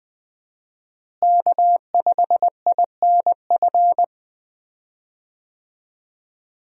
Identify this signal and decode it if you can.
Morse code, heard as a keyed tone, about 20 wpm, 700 Hz. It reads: K5INF
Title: USA Call Signs - 5 Characters - Repeated 3x - 20wpm